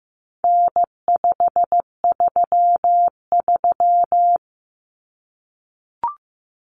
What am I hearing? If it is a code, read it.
Morse code, heard as a keyed tone, about 15 wpm, 700 Hz. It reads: N533
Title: Common Cut Numbers - 4-Digit 15wpm